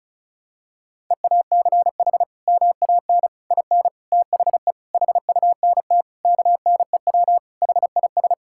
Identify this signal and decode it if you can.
Morse code, heard as a keyed tone, about 36 wpm, 700 Hz. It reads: EACH MAN IN THE HUNT KNEW HIS
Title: Sentences from Top 1600 Words 35wpm